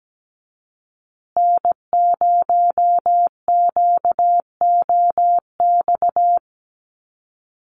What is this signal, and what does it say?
Morse code, heard as a keyed tone, about 17 wpm, 700 Hz. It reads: N0QOX
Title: USA Call Signs Repeated 3x 17wpm